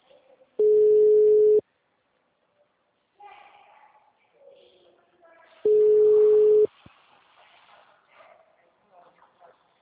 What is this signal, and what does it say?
Ringback tone (Europe, 425 Hz), with speech.
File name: out- -2003- - - .51035